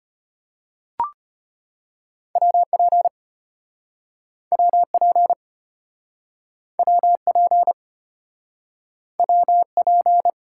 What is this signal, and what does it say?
Morse code, heard as a keyed tone, about 34 wpm, 700 Hz. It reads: WP WP WP WP